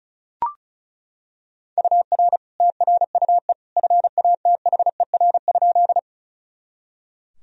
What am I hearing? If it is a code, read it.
Morse code, heard as a keyed tone, about 34 wpm, 700 Hz. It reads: UR TRUE FATHER?
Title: Sentences from Top 400 Words and Abbreviations 35wpm